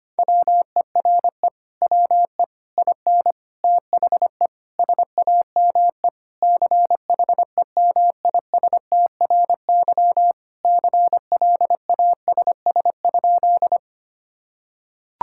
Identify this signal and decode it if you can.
Morse code, heard as a keyed tone, about 25 wpm, 700 Hz. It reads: WERE WE IN THE SAME CHEMISTRY CLASS?
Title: Sentences from Top 1000 Words - Review Part 6 - 25wpm